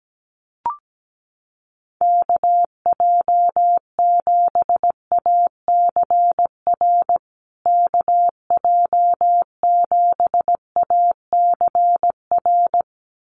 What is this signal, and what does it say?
Morse code, heard as a keyed tone, about 17 wpm, 700 Hz. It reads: KJ7ACR KJ7ACR
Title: USA Call Signs Repeated 2x 17wpm